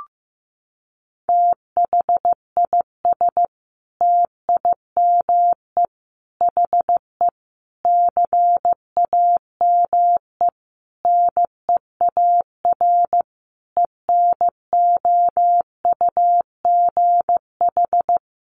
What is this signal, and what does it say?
Morse code, heard as a keyed tone, about 15 wpm, 700 Hz. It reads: THIS TIME HE CAME NEAR ENOUGH
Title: Sentences from Top 1200 Words 15wpm